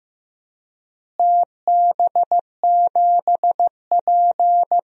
Morse code, heard as a keyed tone, about 15 wpm, 700 Hz. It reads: TB7P
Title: DXCC-100 Call Signs - 4 Characters - Repeated 3x - 15wpm